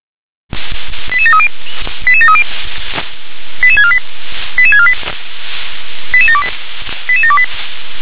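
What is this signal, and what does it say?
Five-tone selective call (ZVEI), heard as quick runs of five tones, tones of about 70 ms. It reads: 88730 88730 88648 88648 88728 88728
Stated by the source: ZVEI five-tone selective call 88081 received with an SDR.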